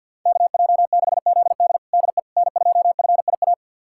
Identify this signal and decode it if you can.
Morse code, heard as a keyed tone, about 52 wpm, 700 Hz. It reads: KJ6ZD DE N2FIA
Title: USA Callsign DE Callsign - 5 Characters - 50wpm